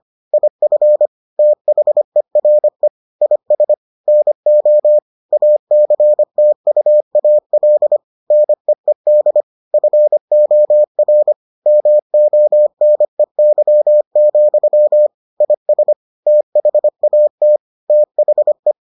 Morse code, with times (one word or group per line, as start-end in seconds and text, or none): 0.3-1.1 IF
1.4-2.9 THERE
3.2-3.7 IS
4.1-5.0 NO
5.3-8.0 ACTUAL
8.3-9.4 NEED
9.7-11.3 FOR
11.7-15.1 MONEY,
15.4-15.9 IS
16.3-17.6 THAT
17.9-18.7 THE